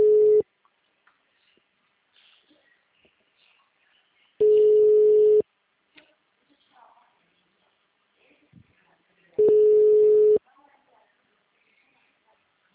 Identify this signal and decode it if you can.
Ringback tone (Europe, 425 Hz), with speech.